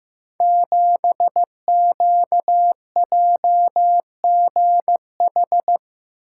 Morse code, heard as a keyed tone, about 15 wpm, 700 Hz. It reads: 7QJGH
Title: Code Groups - 5 Letter:Numbers - 15wpm